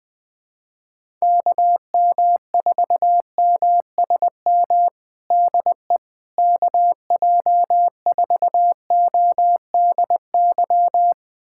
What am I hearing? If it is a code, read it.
Morse code, heard as a keyed tone, about 21 wpm, 700 Hz. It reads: KM4MSM DE KJ4ODY
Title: USA Callsign DE Callsign 20wpm